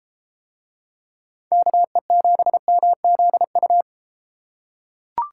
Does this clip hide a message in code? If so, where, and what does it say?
Yes: Morse code from 1.5 to 3.8 s, heard as a keyed tone, about 33 wpm, 700 Hz. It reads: KE7MZU